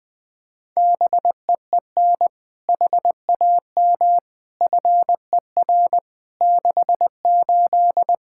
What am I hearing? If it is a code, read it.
Morse code, heard as a keyed tone, about 21 wpm, 700 Hz. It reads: BEEN HAM FER 68
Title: Practice QSO - Exchange 3 - More Info - 20wpm